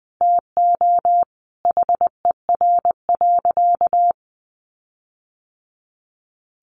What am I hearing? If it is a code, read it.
Morse code, heard as a keyed tone, about 20 wpm, 700 Hz. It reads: TO HER.
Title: Sentences from Top 1300 Words 20wpm